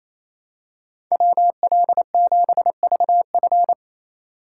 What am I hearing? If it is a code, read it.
Morse code, heard as a keyed tone, about 28 wpm, 700 Hz. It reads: WL7VF
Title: USA Call Signs - 5 Characters - 2x Character Spacing - 28wpm